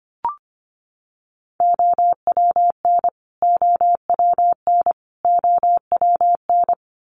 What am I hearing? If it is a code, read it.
Morse code, heard as a keyed tone, about 25 wpm, 700 Hz. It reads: OWN OWN OWN
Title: Top 500 Words Repeated 25wpm